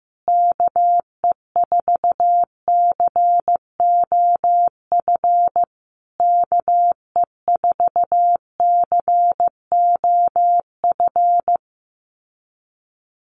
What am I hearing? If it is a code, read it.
Morse code, heard as a keyed tone, about 14 wpm, 700 Hz. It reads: KE4COF KE4COF
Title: USA Call Signs Repeated 3x 15wpm